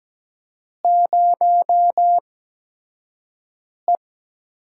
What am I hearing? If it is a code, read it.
Morse code, heard as a keyed tone, about 17 wpm, spaced as for 5 wpm, 700 Hz. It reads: 0E
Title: USA Call Signs - 4 Characters - 8x Character Spacing - 17wpm